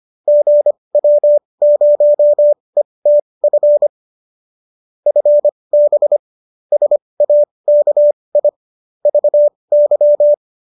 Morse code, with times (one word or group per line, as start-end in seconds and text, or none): 0.3-3.9 GW0ETF
5.1-6.2 FB
6.7-8.5 SAKI
9.1-10.3 VY